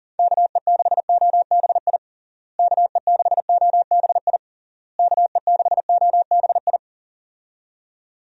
Morse code, heard as a keyed tone, about 37 wpm, 700 Hz. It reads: KE6OBI KE6OBI KE6OBI
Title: USA Call Signs - 6 Characters - Repeated 3x - 40wpm